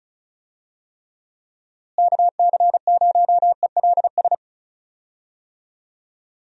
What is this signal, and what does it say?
Morse code, heard as a keyed tone, about 34 wpm, 700 Hz. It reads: KC0ELS
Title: USA Call Signs - 6 Characters - Repeated 3x - 35wpm